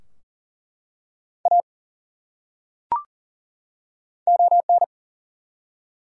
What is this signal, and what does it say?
Morse code, heard as a keyed tone, about 38 wpm, 700 Hz. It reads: A ON
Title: Top 100 Words 40wpm